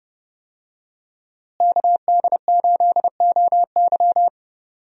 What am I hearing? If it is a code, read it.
Morse code, heard as a keyed tone, about 32 wpm, 700 Hz. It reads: KD8OY
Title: USA Call Signs - 5 Characters - Repeated 3x - 30wpm